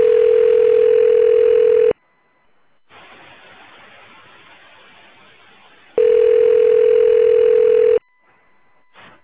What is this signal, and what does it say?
Ringback tone (North America), 440 + 480 Hz.